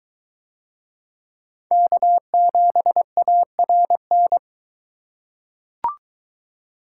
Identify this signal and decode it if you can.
Morse code, heard as a keyed tone, about 22 wpm, 700 Hz. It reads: K7ARN